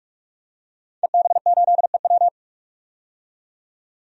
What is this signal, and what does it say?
Morse code, heard as a keyed tone, about 46 wpm, 700 Hz. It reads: EB8EW
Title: DXCC-100 Call Signs - 5 Characters - 45wpm